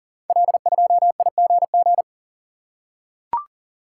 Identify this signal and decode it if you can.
Morse code, heard as a keyed tone, about 38 wpm, 690 Hz. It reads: L2IGG
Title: Code Groups - 5 Letter:Numbers - 40wpm